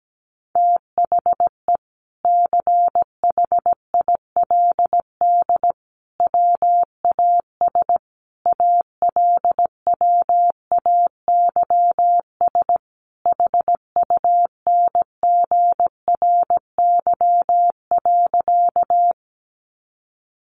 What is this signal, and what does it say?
Morse code, heard as a keyed tone, about 17 wpm, 700 Hz. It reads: THE CHILD WAS ALWAYS HUNGRY.
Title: Sentences from Top 1400 Words 17wpm